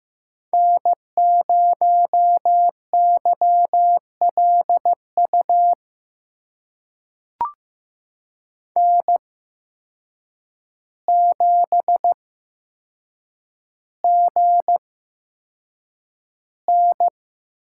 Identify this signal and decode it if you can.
Morse code, heard as a keyed tone, about 15 wpm, spaced as for 7 wpm, 700 Hz. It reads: N0YLU N7GN
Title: USA Call Signs - 5 Characters - 8x Character Spacing - 15wpm